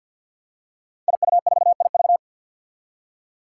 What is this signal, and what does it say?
Morse code, heard as a keyed tone, about 48 wpm, 700 Hz. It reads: IU4IV
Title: Five Challenging Letter-Number Combinations 50wpm